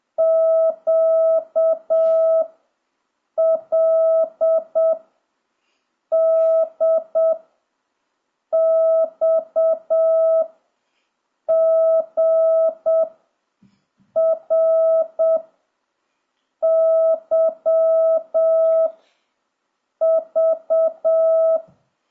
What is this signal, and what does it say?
Morse code, heard as a keyed tone, about 7 wpm, 1300 Hz. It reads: Q L D X G R Y V
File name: jh8cbh / 2026cw